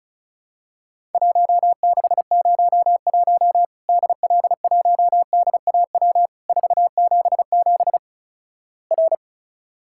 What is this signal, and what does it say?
Morse code, heard as a keyed tone, about 34 wpm, 700 Hz. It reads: 1601 DL1DAW 477 R
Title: Worked All Europe DX Contest - QTC Exchange - 35wpm